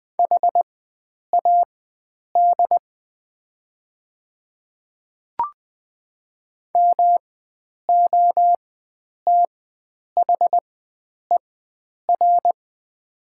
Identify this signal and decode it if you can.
Morse code, heard as a keyed tone, about 20 wpm, spaced as for 9 wpm, 700 Hz. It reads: HAD MOTHER
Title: 4x Character Spacing - Top 300 Words - 20wpm